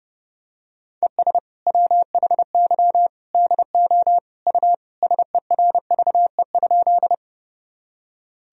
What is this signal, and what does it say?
Morse code, heard as a keyed tone, about 32 wpm, 700 Hz. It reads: ES WHY DO U SERVE?
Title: Sentences from Top 400 Words and Abbreviations 30wpm